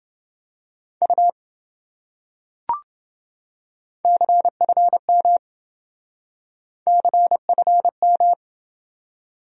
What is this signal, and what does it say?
Morse code, heard as a keyed tone, about 27 wpm, 700 Hz. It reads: U CFM CFM